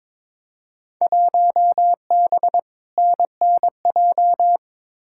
Morse code, heard as a keyed tone, about 22 wpm, 700 Hz. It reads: 1B NNJ
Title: Field Day Exchanges Encore - 22wpm